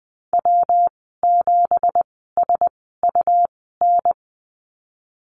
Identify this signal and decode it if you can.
Morse code, heard as a keyed tone, about 20 wpm, spaced as for 15 wpm, 700 Hz. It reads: W7SUN
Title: 179 Lesson 29g - Slash - Call Signs 20wpm